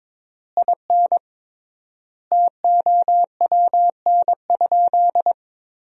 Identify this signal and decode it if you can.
Morse code, heard as a keyed tone, about 22 wpm, 700 Hz. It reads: IN TOWN?
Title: 3x Word-Spacing - Sentences from Top 500 Words - 22wpm